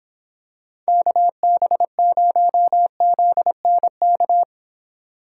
Morse code, heard as a keyed tone, about 26 wpm, 700 Hz. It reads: KB0ZNK